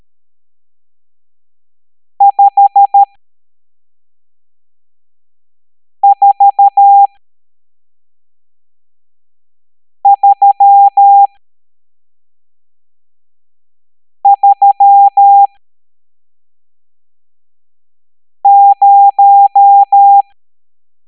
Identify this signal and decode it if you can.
Morse code, heard as a keyed tone, about 13 wpm, spaced as for 3 wpm, 800 Hz. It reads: 54330
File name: hamdata / L12-EX02-CHIFFRES